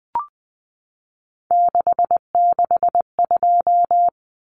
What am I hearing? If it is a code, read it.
Morse code, heard as a keyed tone, about 20 wpm, 700 Hz. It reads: 662
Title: Common Cut Numbers - 3-Digit 20wpm